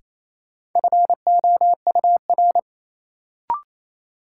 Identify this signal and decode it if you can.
Morse code, heard as a keyed tone, about 28 wpm, 700 Hz. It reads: FOUR
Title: Sentences from Top 300 Words - New Word Drill - 28wpm